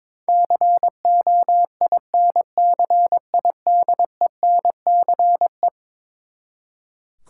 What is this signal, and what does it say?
Morse code, heard as a keyed tone, about 21 wpm, 700 Hz. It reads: COINCIDENCE
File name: practice / 22wpm / Eleven Letter Words - 22wpm